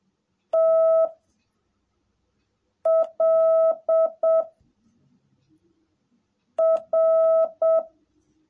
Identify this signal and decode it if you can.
Morse code, heard as a keyed tone, about 7 wpm, 660 Hz. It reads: T L R
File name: jh8cbh / 2021cw / cw18a